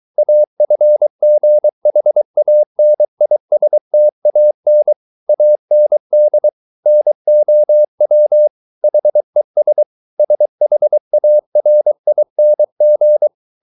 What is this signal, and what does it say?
Morse code, heard as a keyed tone, about 23 wpm, 600 Hz. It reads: AFGHANISTAN AND NOW HES SHARING